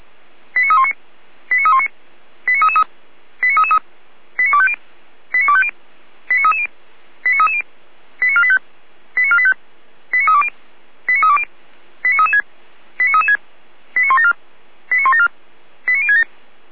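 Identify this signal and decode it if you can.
Five-tone selective call (ZVEI), heard as quick runs of five tones, tones of about 80 ms. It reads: 89319 89319 89333 89333 89250 89250 89330 89330 89485 89485 89320 89320 89336 89336 89174 89174 89067
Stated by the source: ZVEI five-tone selective call received on SDR, from the Feuerwehr Orlamünde fire brigade server.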